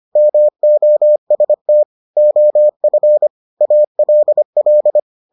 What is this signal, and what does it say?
Morse code, heard as a keyed tone, about 25 wpm, 600 Hz. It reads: MOST OF ALL